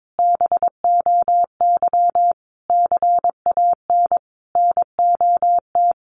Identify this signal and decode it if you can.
Morse code, heard as a keyed tone, about 22 wpm, 700 Hz. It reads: BOY CAN NOT
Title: Sentences from Top 1300 Words 22wpm